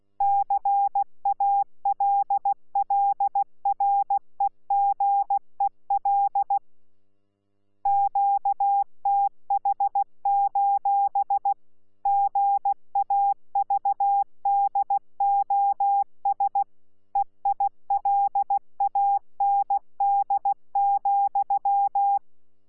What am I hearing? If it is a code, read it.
Morse code, heard as a keyed tone, about 16 wpm, 800 Hz. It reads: CALLREGEL QTH: GAVDOS EILAND,
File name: Z PI4VRZ / PI4VRZ 14-jun-2025